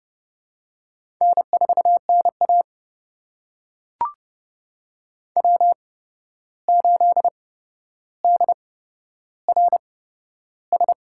Morse code, heard as a keyed tone, about 30 wpm, spaced as for 12 wpm, 700 Hz. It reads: N4NA W8DRS
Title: USA Call Signs - 8x Character Spacing - 30wpm